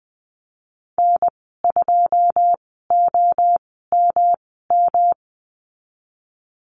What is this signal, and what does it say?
Morse code, heard as a keyed tone, about 20 wpm, spaced as for 14 wpm, 700 Hz. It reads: N2OMM
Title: 122 Lesson 20g - Number 3 - Call Signs 20wpm